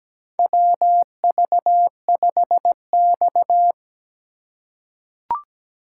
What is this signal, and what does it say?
Morse code, heard as a keyed tone, about 17 wpm, 700 Hz. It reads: WV5X
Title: USA Call Signs - 4 Characters - 3x Character Spacing - 17wpm